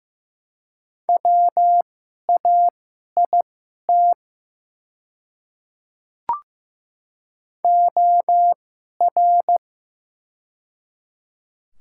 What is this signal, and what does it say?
Morse code, heard as a keyed tone, about 15 wpm, spaced as for 11 wpm, 700 Hz. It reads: WAIT OR
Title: 2x Character Spacing - Top 500 Words - 15wpm